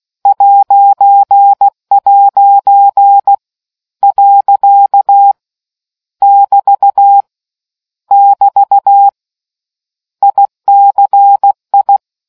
Morse code, heard as a keyed tone, about 16 wpm, 800 Hz. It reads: '' . <BT> <BT> ICI